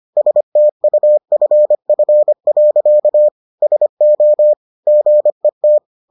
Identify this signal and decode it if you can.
Morse code, heard as a keyed tone, about 25 wpm, 600 Hz. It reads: STUFF. SO GET